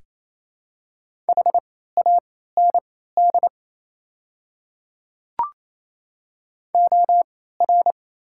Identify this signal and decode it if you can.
Morse code, heard as a keyed tone, about 28 wpm, spaced as for 16 wpm, 690 Hz. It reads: HAND OR
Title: 3x Character Spacing - Top 200 Words - 28wpm